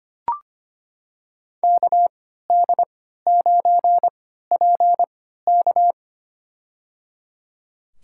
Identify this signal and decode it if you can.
Morse code, heard as a keyed tone, about 25 wpm, spaced as for 14 wpm, 700 Hz. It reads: KD9PK
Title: USA Call Signs - 5 Characters - 3x Character Spacing - 25wpm